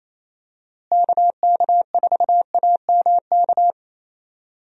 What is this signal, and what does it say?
Morse code, heard as a keyed tone, about 28 wpm, 700 Hz. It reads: KK4AMK
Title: USA Call Signs - 6 Characters - Repeated 3x - 28wpm